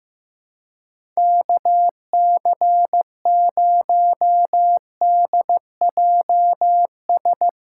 Morse code, heard as a keyed tone, about 15 wpm, 700 Hz. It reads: KC0DJS
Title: USA Call Signs - 6 Characters - 4x Character Spacing - 15wpm